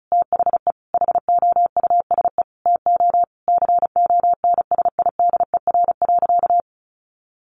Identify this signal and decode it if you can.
Morse code, heard as a keyed tone, about 34 wpm, 700 Hz. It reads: THE HOUSE TO CONSIDER.
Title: Sentences from Top 900 Words 35wpm